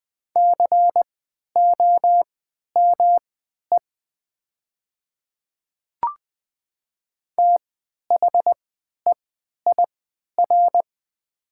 Morse code, heard as a keyed tone, about 20 wpm, spaced as for 11 wpm, 700 Hz. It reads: COME THEIR
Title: 3x Character Spacing - Top 100 Words - 20wpm